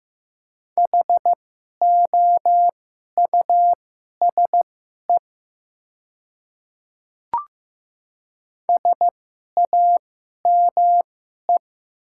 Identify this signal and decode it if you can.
Morse code, heard as a keyed tone, about 15 wpm, spaced as for 11 wpm, 700 Hz. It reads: HOUSE SAME
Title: 2x Character Spacing - Top 200 Words - 15wpm